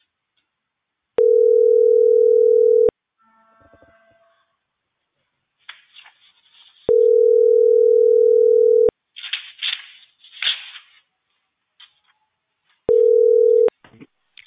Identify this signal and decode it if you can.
Ringback tone (North America), 440 + 480 Hz.